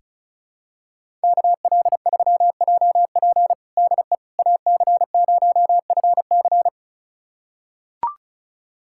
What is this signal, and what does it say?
Morse code, heard as a keyed tone, about 34 wpm, 700 Hz. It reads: KL3JP DE AC0FC
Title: USA Callsign DE Callsign - 5 Characters - 35wpm